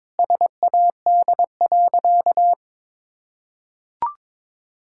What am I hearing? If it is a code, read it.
Morse code, heard as a keyed tone, about 21 wpm, 700 Hz. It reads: SAD.
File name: practice / 22wpm / Sentences with 2-Word Phrases - 22wpm